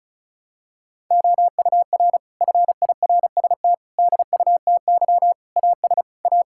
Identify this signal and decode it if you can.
Morse code, heard as a keyed tone, about 36 wpm, 700 Hz. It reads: OUR FIRST DUTY AS A
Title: Sentences from Top 1600 Words 35wpm